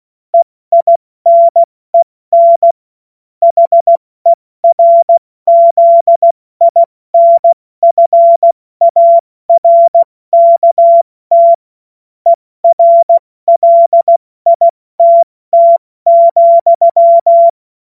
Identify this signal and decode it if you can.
Morse code, heard as a keyed tone, about 16 wpm, 680 Hz. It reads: EINEN HERZINFARKT ERLITT,
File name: Morsekurs / Witze / Witz 20. Dezember 2022 70 F80